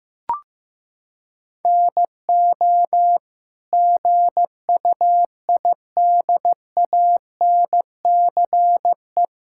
Morse code, heard as a keyed tone, about 15 wpm, 700 Hz. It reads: NO GUIDANCE
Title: Sentences from Top 1600 Words 15wpm